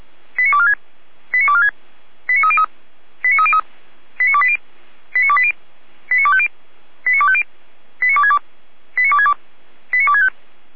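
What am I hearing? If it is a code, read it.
Five-tone selective call (ZVEI), heard as quick runs of five tones, tones of about 70 ms. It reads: 89246 89246 89303 89303 89280 89280 89250 89250 89262 89262 89265
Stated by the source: ZVEI five-tone selective call received on SDR.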